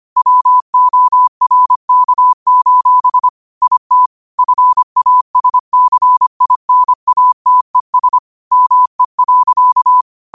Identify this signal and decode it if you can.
Morse code, heard as a keyed tone, about 25 wpm, 1000 Hz. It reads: WORK: IT FASCINATES ME.